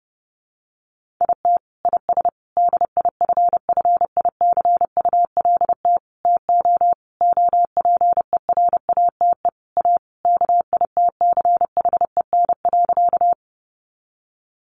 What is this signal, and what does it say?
Morse code, heard as a keyed tone, about 30 wpm, 700 Hz. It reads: IT IS DIFFICULT TO OPERATE A KITCHEN.
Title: Sentences from Top 1500 Words 30wpm